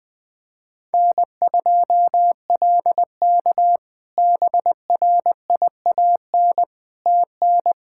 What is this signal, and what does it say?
Morse code, heard as a keyed tone, about 20 wpm, 700 Hz. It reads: N2LK BRIAN TN
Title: K1USN SST QSO Practice - 20wpm